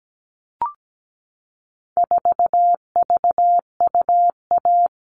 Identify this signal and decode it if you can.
Morse code, heard as a keyed tone, about 17 wpm, 700 Hz. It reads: 4VUA
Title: All Cut Numbers - 4-Digit 17wpm